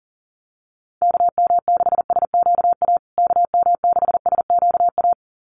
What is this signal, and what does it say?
Morse code, heard as a keyed tone, about 40 wpm, 700 Hz. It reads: KM6SQA KM6SQA
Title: USA Call Signs Repeated 2x 40wpm